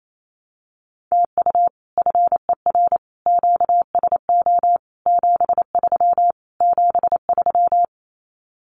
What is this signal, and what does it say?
Morse code, heard as a keyed tone, about 28 wpm, 700 Hz. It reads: TU FER QSO 73 73
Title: Common QSO Phrases Encore 28wpm